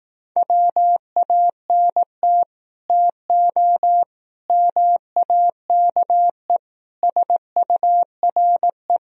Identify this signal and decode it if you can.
Morse code, heard as a keyed tone, about 18 wpm, 700 Hz. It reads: WANT TO MAKE SURE
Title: Speed-Racing - Sets of 5 Words 15wpm